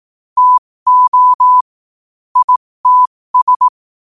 Morse code, heard as a keyed tone, about 18 wpm, spaced as for 15 wpm, 1000 Hz. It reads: TO ITS
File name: 15wpm